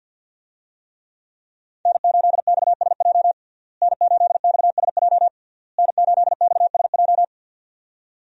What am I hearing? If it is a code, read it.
Morse code, heard as a keyed tone, about 53 wpm, 700 Hz. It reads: N8XSJ N8XSJ N8XSJ
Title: USA Call Signs - 5 Characters - Repeated 3x - 50wpm